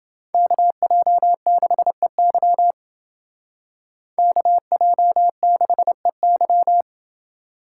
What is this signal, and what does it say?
Morse code, heard as a keyed tone, about 28 wpm, 700 Hz. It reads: KJ6EY KJ6EY